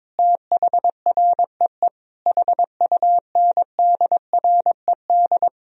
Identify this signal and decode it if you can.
Morse code, heard as a keyed tone, about 22 wpm, 700 Hz. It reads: THREE HUNDRED
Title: Sentences from Top 1000 Words - Review Part 4 - 22wpm